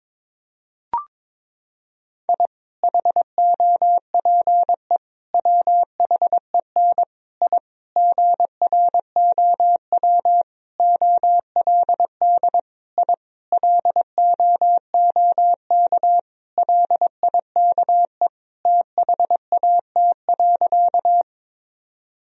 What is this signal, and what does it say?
Morse code, heard as a keyed tone, about 22 wpm, 700 Hz. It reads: I HOPE WHEN I GROW OLD I LOOK LIKE THAT.
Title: Sentences from Top 1000 Words - Review Part 1 - 22wpm